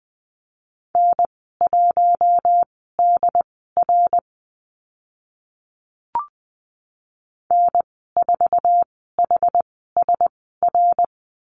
Morse code, heard as a keyed tone, about 20 wpm, spaced as for 14 wpm, 700 Hz. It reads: N1DR N4HSR
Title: 082 Lesson 14g - Number 2 - Call Signs 20wpm